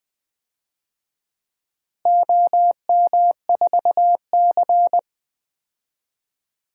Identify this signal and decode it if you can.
Morse code, heard as a keyed tone, about 20 wpm, 700 Hz. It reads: OM4C
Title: DXCC-100 Call Signs - 4 Characters - Repeated 3x - 20wpm